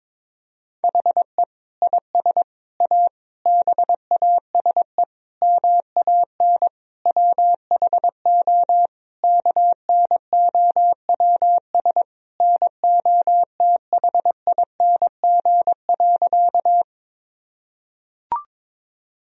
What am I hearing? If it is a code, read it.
Morse code, heard as a keyed tone, about 21 wpm, 700 Hz. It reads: HE IS A BASE MAN WHO KNOWS NOTHING.
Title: Sentences from Top 1000 Words - Review Part 1 - 22wpm